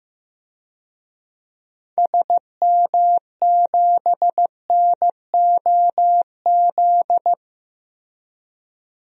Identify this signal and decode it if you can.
Morse code, heard as a keyed tone, about 15 wpm, 700 Hz. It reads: SM7NOZ
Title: DXCC-100 Call Signs - 6 Characters - Repeated 3x - 15wpm